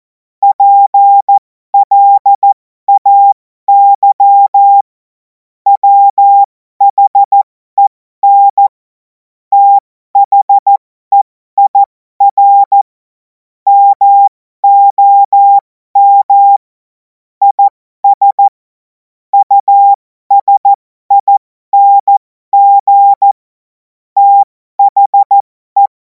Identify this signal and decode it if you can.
Morse code, heard as a keyed tone, about 14 wpm, spaced as for 12 wpm, 800 Hz. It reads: PLAY WHEN THEIR MOM IS USING THE
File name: Morsekurs / Witze